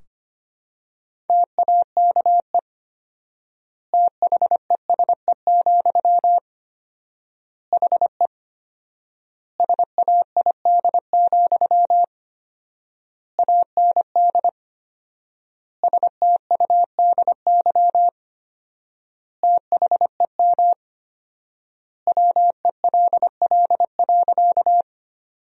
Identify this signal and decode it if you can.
Morse code, heard as a keyed tone, about 25 wpm, 700 Hz. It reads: TAKE THESE, HE SAID, AND STUDY THEM WELL.
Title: 4x Word-Spacing - Sentences from Top 300 Words - 25wpm